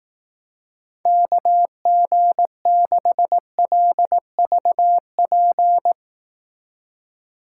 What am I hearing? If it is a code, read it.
Morse code, heard as a keyed tone, about 18 wpm, 700 Hz. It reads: KG6LVP
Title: Speed Racing - USA 2x3 Call Signs 15wpm